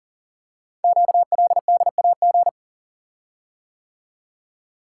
Morse code, heard as a keyed tone, about 42 wpm, 700 Hz. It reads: QLDAG
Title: Code Groups - 5 Letters - 40wpm